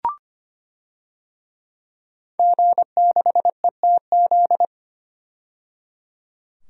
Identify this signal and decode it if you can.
Morse code, heard as a keyed tone, about 25 wpm, 700 Hz. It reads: G6ETZ
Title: DXCC-100 Call Signs - 5 Characters - 25wpm